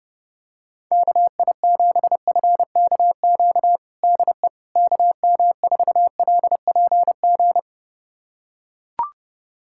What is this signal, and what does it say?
Morse code, heard as a keyed tone, about 30 wpm, 700 Hz. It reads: KI7FKQ DE KM4LPG
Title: USA Callsign DE Callsign - 6 Characters - 30wpm